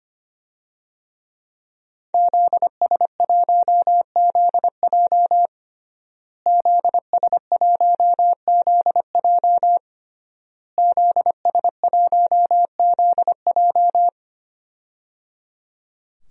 Morse code, heard as a keyed tone, about 25 wpm, 700 Hz. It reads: ZS1ZJ ZS1ZJ ZS1ZJ
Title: DXCC-100 Call Signs - 5 Characters - Repeated 3x - 25wpm